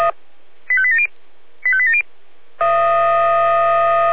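Five-tone selective call (ZVEI), heard as quick runs of five tones, tones of about 70 ms. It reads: 85870 85870
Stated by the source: ZVEI five-tone selective call received on an SDR, channel 496.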